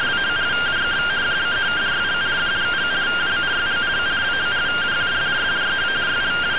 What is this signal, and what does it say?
Two-tone teleprinter signal, about 170 Hz shift.
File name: findings / MR AXIAL 0mm 6mm.gen-02 rtty rx